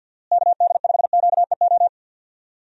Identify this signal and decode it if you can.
Morse code, heard as a keyed tone, about 50 wpm, 700 Hz. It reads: KD5QEO
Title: USA Call Signs - 6 Characters - Repeated 3x - 50wpm